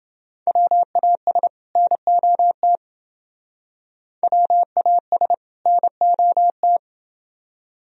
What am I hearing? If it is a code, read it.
Morse code, heard as a keyed tone, about 28 wpm, 700 Hz. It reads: WAS NOT WAS NOT